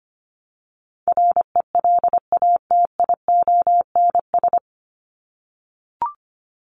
Morse code, heard as a keyed tone, about 25 wpm, 700 Hz. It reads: RELATIONS
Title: ES:S Suffix - Top 100 Words with Base Words 25wpm